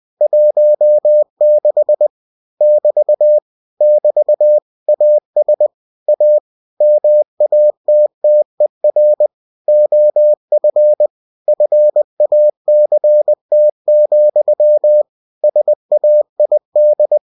Morse code, heard as a keyed tone, about 20 wpm, 600 Hz. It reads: 16 <BT> <BT> AS A MATTER OF FACT, SAID
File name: cw-books / winnie-the-pooh / winnie-20wpm-page-0016